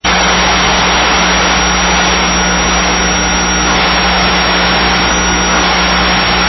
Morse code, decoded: none